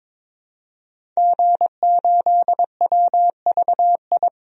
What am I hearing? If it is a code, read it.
Morse code, heard as a keyed tone, about 22 wpm, 700 Hz. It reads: G8WVI
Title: DXCC-100 Call Signs - 5 Characters - 22wpm